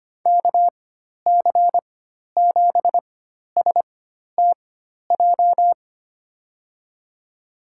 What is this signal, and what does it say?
Morse code, heard as a keyed tone, about 25 wpm, spaced as for 12 wpm, 700 Hz. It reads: KC7STJ
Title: USA Call Signs - 6 Characters - 4x Character Spacing - 25wpm